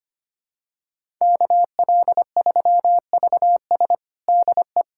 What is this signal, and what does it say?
Morse code, heard as a keyed tone, about 25 wpm, 700 Hz. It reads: KL3VS DE
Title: USA Callsign DE Callsign - 5 Characters - 25wpm